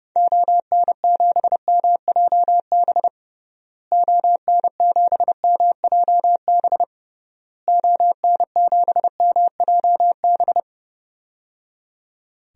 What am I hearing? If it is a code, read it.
Morse code, heard as a keyed tone, about 30 wpm, 700 Hz. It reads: ON7MJB ON7MJB ON7MJB
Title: DXCC-100 Call Signs - 6 Characters - Repeated 3x - 30wpm